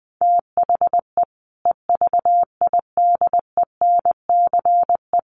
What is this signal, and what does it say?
Morse code, heard as a keyed tone, about 20 wpm, 700 Hz. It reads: THE EVIDENCE
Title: Sentences from Top 1300 Words 20wpm